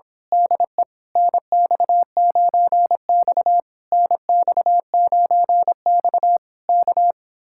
Morse code, heard as a keyed tone, about 26 wpm, 700 Hz. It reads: DE NX9X NX9X K
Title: Speed-Racing Common QSO Phrases Encore - 22wpm